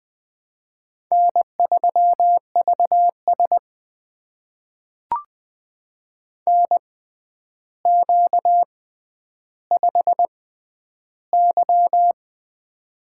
Morse code, heard as a keyed tone, about 20 wpm, spaced as for 10 wpm, 700 Hz. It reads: N3VS NQ5Y
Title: USA Call Signs - 4 Characters - 6x Character Spacing - 20wpm